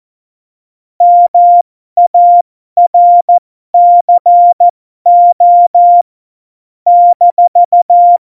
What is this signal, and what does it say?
Morse code, heard as a keyed tone, about 14 wpm, spaced as for 12 wpm, 700 Hz. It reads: MARCO -
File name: Morsekurs / Witze